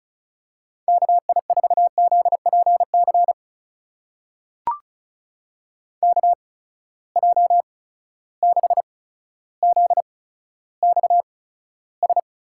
Morse code, heard as a keyed tone, about 34 wpm, spaced as for 15 wpm, 700 Hz. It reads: KI4ZPC KJ6ZXS
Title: USA Call Signs - 6 Characters - 8x Character Spacing - 35wpm